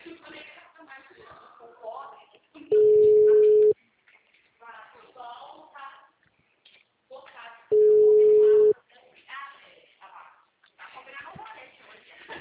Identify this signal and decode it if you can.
Ringback tone (Europe, 425 Hz), with speech.